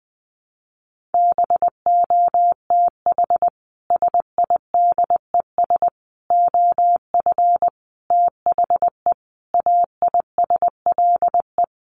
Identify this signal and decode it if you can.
Morse code, heard as a keyed tone, about 19 wpm, 700 Hz. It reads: BOTH SIDES OF THE AISLE
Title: Sets of 5 Words Encore2 20wpm